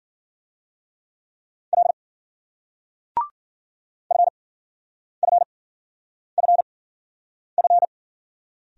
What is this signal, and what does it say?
Morse code, heard as a keyed tone, about 46 wpm, spaced as for 9 wpm, 700 Hz. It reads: L FFFF
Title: Speed-Racing Single Letters 40wpm